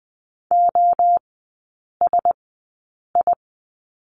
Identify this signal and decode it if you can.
Morse code, heard as a keyed tone, about 20 wpm, spaced as for 8 wpm, 700 Hz. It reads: OSI
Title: 035 Lesson 7e - Letter s - Three Characters 20wpm